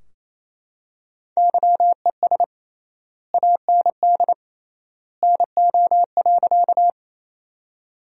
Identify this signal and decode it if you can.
Morse code, heard as a keyed tone, about 28 wpm, 700 Hz. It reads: YES AND NO.
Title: 3x Word-Spacing - Sentences from Top 500 Words - 28wpm